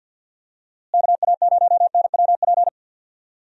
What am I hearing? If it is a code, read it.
Morse code, heard as a keyed tone, about 50 wpm, 700 Hz. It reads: KA0NWP
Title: USA Call Signs - 6 Characters - Repeated 2x - 50wpm